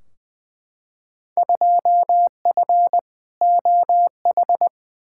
Morse code, heard as a keyed tone, about 20 wpm, 700 Hz. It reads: 2F OH
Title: Field Day Exchanges - 20wpm